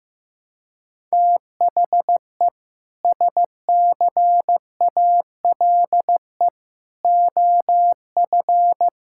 Morse code, heard as a keyed tone, about 15 wpm, 700 Hz. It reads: THE SCALE OF
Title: Sentences from Top 1000 Words - Review Part 5 - 15wpm